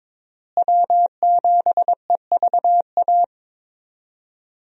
Morse code, heard as a keyed tone, about 22 wpm, 700 Hz. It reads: W7EVA
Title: USA Call Signs - 5 Characters - Repeated 3x - 22wpm